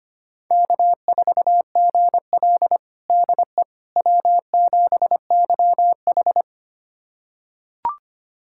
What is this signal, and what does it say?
Morse code, heard as a keyed tone, about 25 wpm, 700 Hz. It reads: K4GL DE W7YH
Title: USA Callsign DE Callsign - 4 Characters - 25wpm